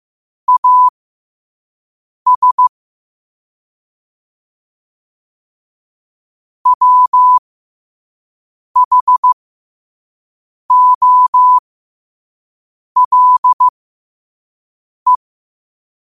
Morse code, heard as a keyed tone, about 15 wpm, spaced as for 5 wpm, 1000 Hz. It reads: AS WHOLE
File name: qotdfiles / 5wpm